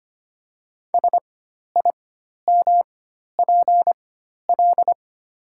Morse code, heard as a keyed tone, about 25 wpm, spaced as for 12 wpm, 700 Hz. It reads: SIMPL
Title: 4x Character Spacing - Top 500 Words - 25wpm